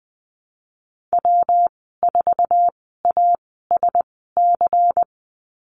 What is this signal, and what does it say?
Morse code, heard as a keyed tone, about 20 wpm, spaced as for 14 wpm, 700 Hz. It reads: W4ASC
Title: 122 Lesson 20g - Number 3 - Call Signs 20wpm